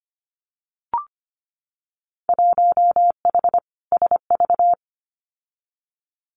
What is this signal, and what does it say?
Morse code, heard as a keyed tone, about 25 wpm, 700 Hz. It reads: 1H SV
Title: Winter Field Day Exchanges 25wpm